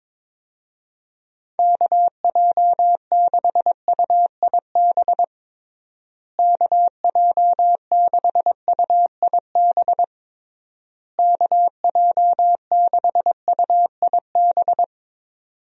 Morse code, heard as a keyed tone, about 22 wpm, 700 Hz. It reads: KJ6UIB KJ6UIB KJ6UIB
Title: USA Call Signs - 6 Characters - Repeated 3x - 22wpm